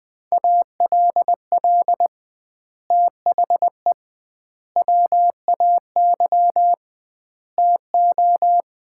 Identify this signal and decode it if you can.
Morse code, heard as a keyed tone, about 20 wpm, 700 Hz. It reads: ALL THE WAY TO
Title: 2x Word-Spacing - Sets of 4 Words - 20wpm